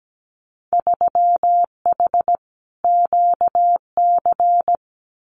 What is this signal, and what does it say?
Morse code, heard as a keyed tone, about 17 wpm, 700 Hz. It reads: 3H QC
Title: Winter Field Day Exchanges Encore 17wpm